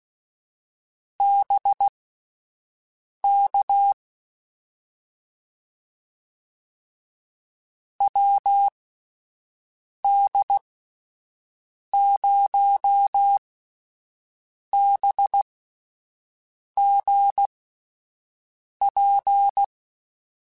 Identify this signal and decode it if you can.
Morse code, heard as a keyed tone, about 16 wpm, spaced as for 5 wpm, 800 Hz. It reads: BK WD0BGP
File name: QSOs / QSOs-mp3s / WD0BGP 1@13